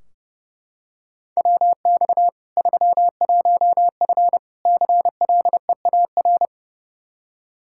Morse code, heard as a keyed tone, about 30 wpm, 700 Hz. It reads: WX 31F CLEAR
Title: Practice QSO - Exchange 2 - Info - 30wpm